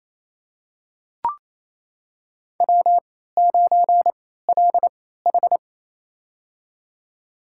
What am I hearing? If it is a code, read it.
Morse code, heard as a keyed tone, about 28 wpm, spaced as for 16 wpm, 700 Hz. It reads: W9LH
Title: USA Call Signs - 4 Characters - 3x Character Spacing - 28wpm